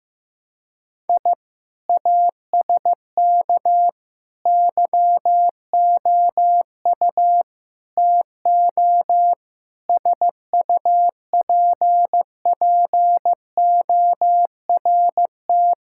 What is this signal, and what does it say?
Morse code, heard as a keyed tone, about 15 wpm, 700 Hz. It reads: I ASK YOU TO SUPPORT
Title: Sentences from Top 1000 Words - Review Part 5 - 15wpm